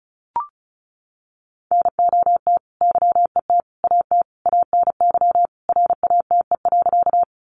Morse code, heard as a keyed tone, about 34 wpm, 700 Hz. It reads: NOT YET AT ANY RATE.
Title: Sentences from Top 700 Words 35wpm